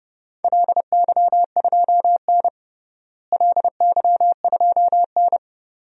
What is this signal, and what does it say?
Morse code, heard as a keyed tone, about 32 wpm, 700 Hz. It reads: LY2N LY2N
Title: DXCC-100 Call Signs - 4 Characters - Repeated 2x - 30wpm